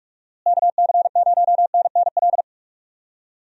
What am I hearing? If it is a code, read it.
Morse code, heard as a keyed tone, about 46 wpm, 700 Hz. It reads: KC0NNL
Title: USA Call Signs - 6 Characters - Repeated 3x - 45wpm